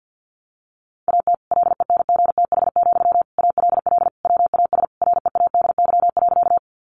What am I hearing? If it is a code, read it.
Morse code, heard as a keyed tone, about 49 wpm, 700 Hz. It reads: AT LENGTH, ALL WAS READY.
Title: Sentences from Top 1000 Words 50wpm